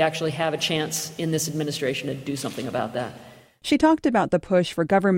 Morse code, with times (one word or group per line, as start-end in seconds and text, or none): none